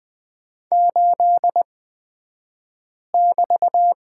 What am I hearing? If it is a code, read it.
Morse code, heard as a keyed tone, about 20 wpm, spaced as for 5 wpm, 690 Hz. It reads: 8<BT>
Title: 235 Lesson 38d - Letter q - Two Characters 20wpm